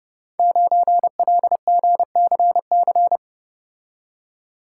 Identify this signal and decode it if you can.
Morse code, heard as a keyed tone, about 30 wpm, 700 Hz. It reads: 9LGCC
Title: Code Groups - 5 Letter:Numbers - 30wpm